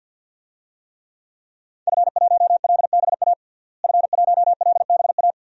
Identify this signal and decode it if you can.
Morse code, heard as a keyed tone, about 47 wpm, 700 Hz. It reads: F1LBA F1LBA
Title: DXCC-100 Call Signs - 5 Characters - Repeated 2x - 50wpm